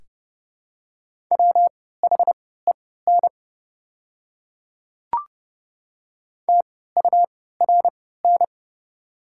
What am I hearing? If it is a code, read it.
Morse code, heard as a keyed tone, about 30 wpm, spaced as for 17 wpm, 700 Hz. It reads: WHEN TURN
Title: 3x Character Spacing - Top 200 Words - 30wpm